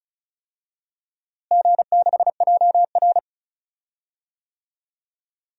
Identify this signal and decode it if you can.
Morse code, heard as a keyed tone, about 34 wpm, 700 Hz. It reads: G6JR
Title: DXCC-100 Call Signs - 4 Characters - Repeated 3x - 35wpm